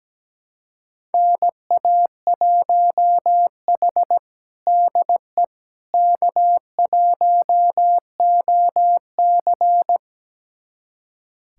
Morse code, heard as a keyed tone, about 17 wpm, 700 Hz. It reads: NA1H DE K1OC
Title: USA Callsign DE Callsign - 4 Characters - 17wpm